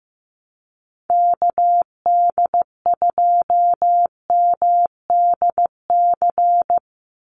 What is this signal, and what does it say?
Morse code, heard as a keyed tone, about 15 wpm, 700 Hz. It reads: KD2MDC
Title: USA Call Signs 15wpm